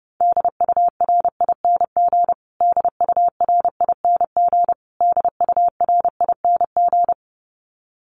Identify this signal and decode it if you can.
Morse code, heard as a keyed tone, about 29 wpm, 700 Hz. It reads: DURING DURING DURING
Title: Top 500 Words Repeated 30wpm